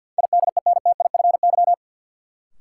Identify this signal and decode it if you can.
Morse code, heard as a keyed tone, about 51 wpm, 700 Hz. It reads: IDENTIFY
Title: Sentences from Top 900 Words - New Word Drill - 50wpm